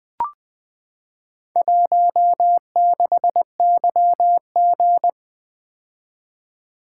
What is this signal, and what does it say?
Morse code, heard as a keyed tone, about 20 wpm, 700 Hz. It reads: 16YG
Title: Four Letter-Number Combinations - 20wpm